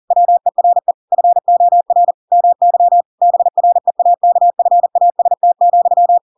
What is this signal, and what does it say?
Morse code, heard as a keyed tone, about 40 wpm, 700 Hz. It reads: WERE FOR MY BREAKFAST,
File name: Pinocchio / Pinocchio ch7 40 40 ik1hge